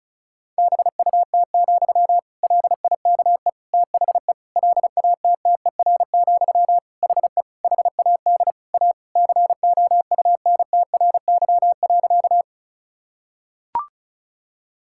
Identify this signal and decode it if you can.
Morse code, heard as a keyed tone, about 36 wpm, 700 Hz. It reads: BUT, LIKE THE LATTER, HE HAD A COUNTRY.
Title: Sentences from Top 1600 Words 35wpm